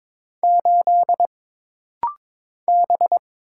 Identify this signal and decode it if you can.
Morse code, heard as a keyed tone, about 22 wpm, 700 Hz. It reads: 8 B
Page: This rapid-fire timing is geared to push your ICR (Instant Character Recognition) skills.